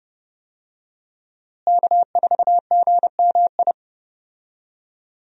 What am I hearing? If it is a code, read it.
Morse code, heard as a keyed tone, about 30 wpm, 700 Hz. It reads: K4GMI
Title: USA Call Signs - 5 Characters - Repeated 2x - 30wpm